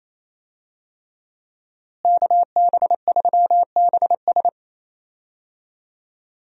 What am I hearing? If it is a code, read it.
Morse code, heard as a keyed tone, about 27 wpm, 700 Hz. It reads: KB3BS
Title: USA Call Signs - 5 Characters - Repeated 3x - 28wpm